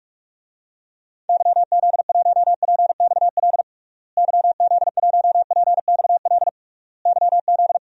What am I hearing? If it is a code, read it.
Morse code, heard as a keyed tone, about 46 wpm, 700 Hz. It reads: YZ1PXL YZ1PXL YZ
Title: DXCC-100 Call Signs - 6 Characters - Repeated 3x - 45wpm